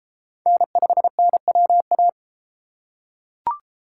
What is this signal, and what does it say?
Morse code, heard as a keyed tone, about 32 wpm, 700 Hz. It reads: N5NWA